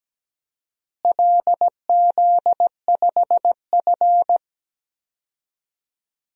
Morse code, heard as a keyed tone, about 17 wpm, 700 Hz. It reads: LZ5F
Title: DXCC-100 Call Signs - 4 Characters - Repeated 3x - 17wpm